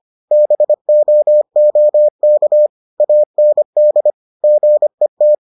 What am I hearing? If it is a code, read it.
Morse code, heard as a keyed tone, about 25 wpm, 600 Hz. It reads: BOOK AND GET